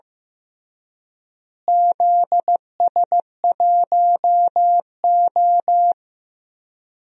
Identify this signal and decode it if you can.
Morse code, heard as a keyed tone, about 15 wpm, 700 Hz. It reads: ZS1O
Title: DXCC-100 Call Signs - 4 Characters - Repeated 3x - 15wpm